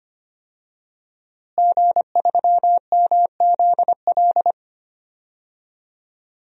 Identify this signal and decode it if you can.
Morse code, heard as a keyed tone, about 25 wpm, 700 Hz. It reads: G3MZL
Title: DXCC-100 Call Signs - 5 Characters - Repeated 3x - 25wpm